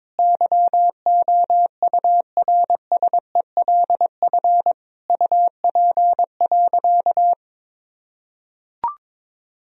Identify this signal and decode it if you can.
Morse code, heard as a keyed tone, about 21 wpm, 700 Hz. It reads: YOURSELF UP.
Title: Sentences from Top 1000 Words - Review Part 1 - 22wpm